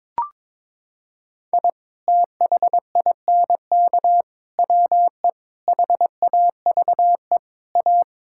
Morse code, heard as a keyed tone, about 21 wpm, 700 Hz. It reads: I THINK WE HAVE A
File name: practice / 22wpm / Sentences with 4-Word Phrases - 22wpm